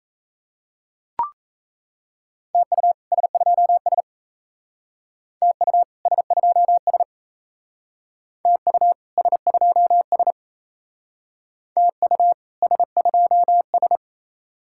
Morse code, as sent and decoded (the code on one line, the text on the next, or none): - ..- / ... ..--- ... / - ..- / ... ..--- ... / - ..- / ... ..--- ... / - ..- / ... ..--- ...
TU S2S TU S2S TU S2S TU S2S